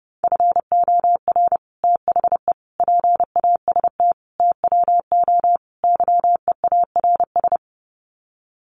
Morse code, heard as a keyed tone, about 30 wpm, 700 Hz. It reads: FOR THE PAST TWO YEARS
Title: Sets of 5 Words Encore2 30wpm